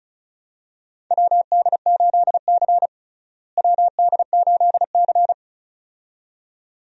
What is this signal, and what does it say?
Morse code, heard as a keyed tone, about 36 wpm, 700 Hz. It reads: WD8C WD8C
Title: USA Call Signs - 4 Characters - Repeated 2x - 35wpm